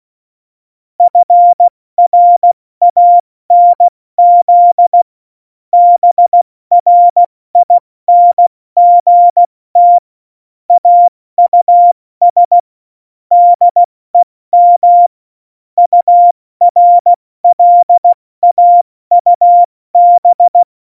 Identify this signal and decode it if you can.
Morse code, heard as a keyed tone, about 16 wpm, 700 Hz. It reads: FRANZ BRINGT AUS DEM URLAUB
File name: Morsekurs / Witze / Witz 04. Dezember, 70 F80